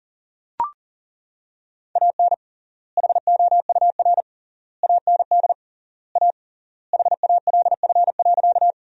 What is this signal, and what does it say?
Morse code, heard as a keyed tone, about 40 wpm, 700 Hz. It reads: AN HOUR AND A HALF.
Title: 3x Word-Spacing - Sentences from Top 400 Words - 40wpm